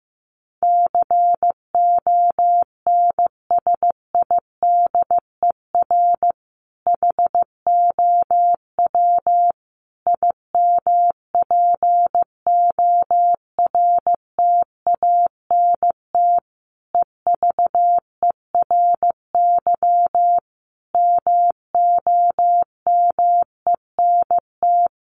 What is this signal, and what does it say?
Morse code, heard as a keyed tone, about 15 wpm, 700 Hz. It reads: CONSIDER HOW IMPORTANT EVERY MOMENT
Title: Sentences from Top 1000 Words 15wpm